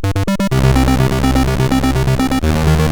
There is no code.